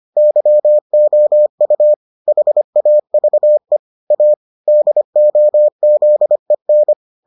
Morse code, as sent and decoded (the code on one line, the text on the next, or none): -.-- --- ..- / .... .- ...- . / .- / -.. --- --.. -.
YOU HAVE A DOZEN